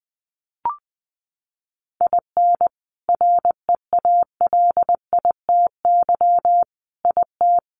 Morse code, heard as a keyed tone, about 19 wpm, 700 Hz. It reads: IN REALITY IT
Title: Sentences from Top 1200 Words 20wpm